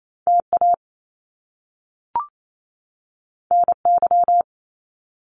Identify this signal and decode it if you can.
Morse code, heard as a keyed tone, about 28 wpm, 700 Hz. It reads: T A N Y
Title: Top Two Letter Combinations in Words 28wpm